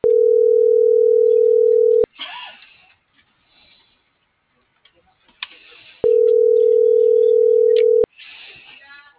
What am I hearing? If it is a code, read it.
Ringback tone (North America), 440 + 480 Hz.